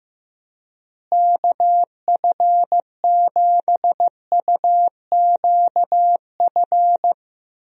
Morse code, heard as a keyed tone, about 15 wpm, 700 Hz. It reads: KF7UQF
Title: USA Call Signs - 6 Characters - Repeated 2x - 15wpm